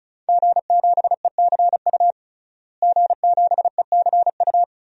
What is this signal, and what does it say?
Morse code, heard as a keyed tone, about 36 wpm, 700 Hz. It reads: G7ECU G7ECU
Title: DXCC-100 Call Signs - 5 Characters - Repeated 3x - 35wpm